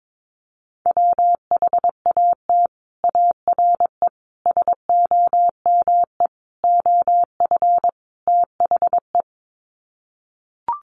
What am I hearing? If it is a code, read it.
Morse code, heard as a keyed tone, about 22 wpm, 700 Hz. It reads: WHAT ARE SOME OF THE
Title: Sets of 5 Words Encore 22wpm